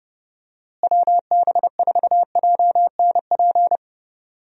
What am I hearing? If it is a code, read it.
Morse code, heard as a keyed tone, about 30 wpm, 700 Hz. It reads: WB4JNP
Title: USA Call Signs - 6 Characters - Repeated 3x - 30wpm